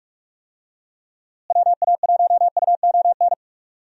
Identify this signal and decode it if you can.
Morse code, heard as a keyed tone, about 46 wpm, 700 Hz. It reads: WA1UON
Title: USA Call Signs - 6 Characters - Repeated 2x - 45wpm